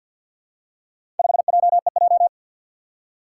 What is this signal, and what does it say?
Morse code, heard as a keyed tone, about 47 wpm, 700 Hz. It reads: 5JEJ
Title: Four Challenging Letter-Number Combinations 50wpm